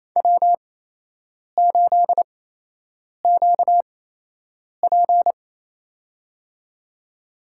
Morse code, heard as a keyed tone, about 28 wpm, spaced as for 8 wpm, 700 Hz. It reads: W8QP
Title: USA Call Signs - 4 Characters - 8x Character Spacing - 28wpm